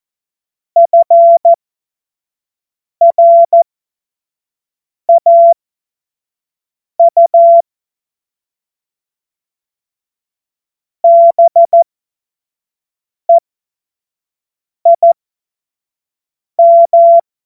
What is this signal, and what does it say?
Morse code, heard as a keyed tone, about 14 wpm, spaced as for 5 wpm, 680 Hz. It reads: FRAU BEIM